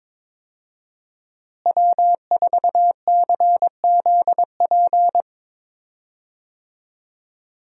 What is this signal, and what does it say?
Morse code, heard as a keyed tone, about 22 wpm, 700 Hz. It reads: W4CZP
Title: USA Call Signs - 5 Characters - Repeated 3x - 22wpm